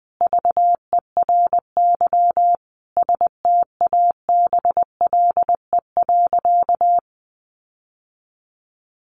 Morse code, heard as a keyed tone, about 20 wpm, 700 Hz. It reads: VERY STABLE.
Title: Sentences from Top 1500 Words 20wpm